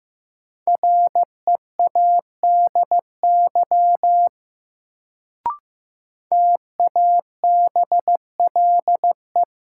Morse code, heard as a keyed tone, about 15 wpm, 700 Hz. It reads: READY TABLE
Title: Sentences from Top 400 Words - New Word Drill - 15wpm